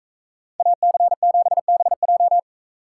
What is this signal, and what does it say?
Morse code, heard as a keyed tone, about 41 wpm, 700 Hz. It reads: AC7BJ